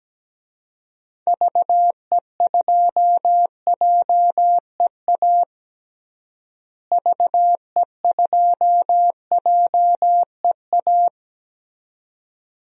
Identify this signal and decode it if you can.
Morse code, heard as a keyed tone, about 17 wpm, 700 Hz. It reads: VE2JEA VE2JEA
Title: DXCC-100 Call Signs - 6 Characters - Repeated 2x - 17wpm